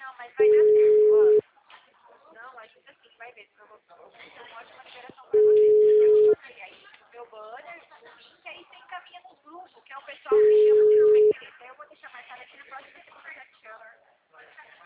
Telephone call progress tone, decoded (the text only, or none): ringback tone